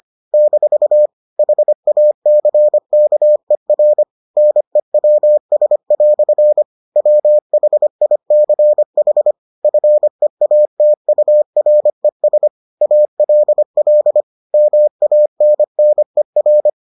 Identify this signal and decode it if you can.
Morse code, heard as a keyed tone, about 25 wpm, 600 Hz. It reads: - HACKER NEWS" WHICH FEATURES ALL MANNER